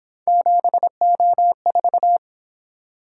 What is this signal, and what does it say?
Morse code, heard as a keyed tone, about 27 wpm, 700 Hz. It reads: 7O4
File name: Speed-Racing - Three Letter-Number Combinations - 22wpm